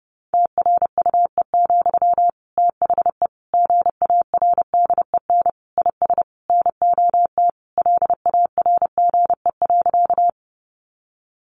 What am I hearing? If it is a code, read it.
Morse code, heard as a keyed tone, about 30 wpm, 700 Hz. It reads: TRUE, THE GARDEN IS NOT LARGE.
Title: Sentences from Top 700 Words 30wpm